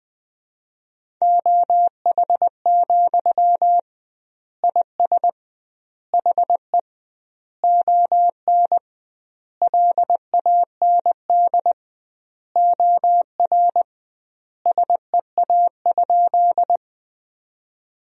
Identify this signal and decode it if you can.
Morse code, heard as a keyed tone, about 20 wpm, 700 Hz. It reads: OH, IS HE ON LAND OR SEA?
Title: 2x Word-Spacing - Sentences from Top 500 Words - 20wpm